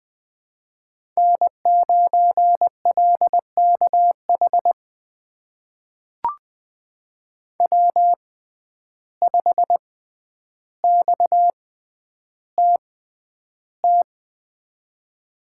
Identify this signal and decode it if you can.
Morse code, heard as a keyed tone, about 20 wpm, spaced as for 10 wpm, 700 Hz. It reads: N9LKH W5XTT
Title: USA Call Signs - 5 Characters - 6x Character Spacing - 20wpm